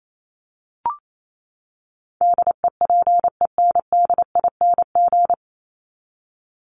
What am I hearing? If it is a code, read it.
Morse code, heard as a keyed tone, about 28 wpm, 700 Hz. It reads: DEPENDING
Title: ING Suffix - Top 100 Words with Base Words 28wpm